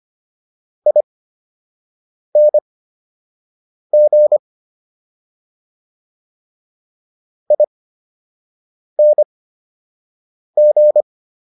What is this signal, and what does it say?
Morse code, heard as a keyed tone, about 25 wpm, spaced as for 6 wpm, 600 Hz. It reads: ING ING